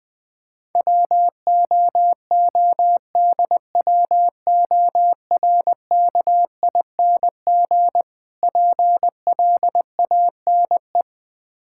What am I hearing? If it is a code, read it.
Morse code, heard as a keyed tone, about 20 wpm, 700 Hz. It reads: WOODWORKING PLANE
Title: Difficult Words - Speed-Racing with Hints - 20wpm